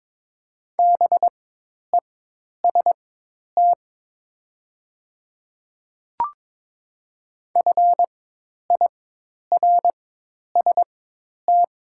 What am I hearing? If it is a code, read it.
Morse code, heard as a keyed tone, about 23 wpm, spaced as for 10 wpm, 700 Hz. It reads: BEST FIRST
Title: 4x Character Spacing - Top 500 Words - 22wpm